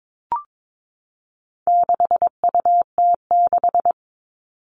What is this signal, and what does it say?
Morse code, heard as a keyed tone, about 22 wpm, 700 Hz. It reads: 6UT6
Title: All Cut Numbers - 4-Digit 22wpm